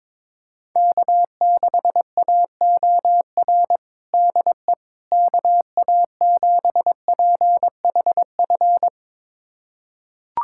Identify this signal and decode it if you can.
Morse code, heard as a keyed tone, about 22 wpm, 700 Hz. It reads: K6AOR DE KA7PHF
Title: USA Callsign DE Callsign 22wpm